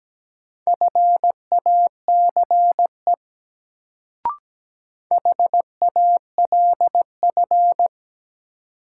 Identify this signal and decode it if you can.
Morse code, heard as a keyed tone, about 17 wpm, 700 Hz. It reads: FACE HALF
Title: Sets of 3 Words - New Word Drill - 17wpm